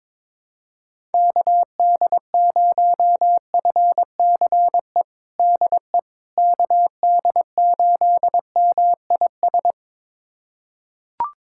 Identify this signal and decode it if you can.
Morse code, heard as a keyed tone, about 22 wpm, 700 Hz. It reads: KD0FCE DE KD8MIS
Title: USA Callsign DE Callsign 22wpm